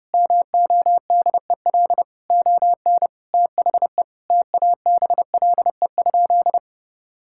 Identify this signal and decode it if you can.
Morse code, heard as a keyed tone, about 32 wpm, 700 Hz. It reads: MODEL ON THE TABLE?
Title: Sentences from Top 1000 Words - Review Part 5 - 30wpm